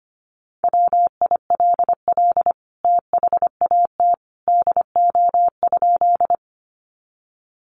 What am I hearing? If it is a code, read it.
Morse code, heard as a keyed tone, about 25 wpm, 700 Hz. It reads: WILL THAT DO?
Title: Sentences from Top 100 Words 25wpm